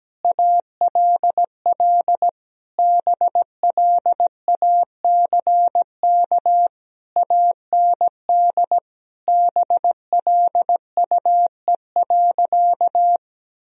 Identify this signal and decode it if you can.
Morse code, heard as a keyed tone, about 17 wpm, 700 Hz. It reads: ALL BLACK AND BLUE.
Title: Sentences from Top 1000 Words - Review Part 3 - 17wpm